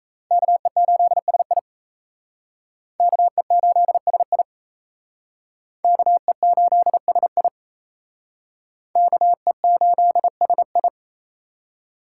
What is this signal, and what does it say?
Morse code, heard as a keyed tone, about 36 wpm, 700 Hz. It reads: KE8SI KE8SI KE8SI KE8SI